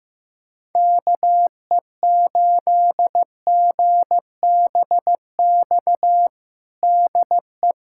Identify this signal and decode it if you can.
Morse code, heard as a keyed tone, about 15 wpm, 700 Hz. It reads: KE8GBX DE
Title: USA Callsign DE Callsign - 6 Characters - 15wpm